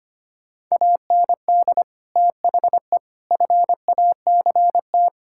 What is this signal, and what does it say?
Morse code, heard as a keyed tone, about 25 wpm, 700 Hz. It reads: AND THE FACT
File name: practice / 25wpm / Sentences with 4-Word Phrases - 25wpm